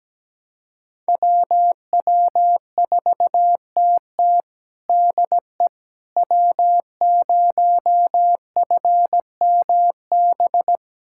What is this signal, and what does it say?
Morse code, heard as a keyed tone, about 17 wpm, 700 Hz. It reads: WW4TT DE W0FMB
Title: USA Callsign DE Callsign - 5 Characters - 17wpm